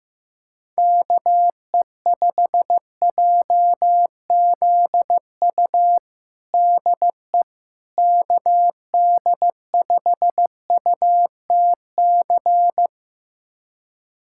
Morse code, heard as a keyed tone, about 16 wpm, 700 Hz. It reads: KE5JZU DE KD5UTC
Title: USA Callsign DE Callsign 15wpm